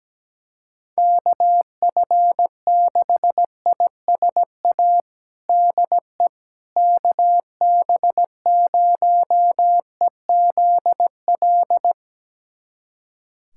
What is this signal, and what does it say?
Morse code, heard as a keyed tone, about 17 wpm, 700 Hz. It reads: KF6ISA DE KB0EZL
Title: USA Callsign DE Callsign - 6 Characters - 17wpm